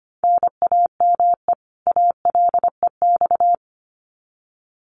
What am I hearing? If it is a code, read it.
Morse code, heard as a keyed tone, about 25 wpm, 700 Hz. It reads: NAME ALEX
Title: Common QSO Phrases 25wpm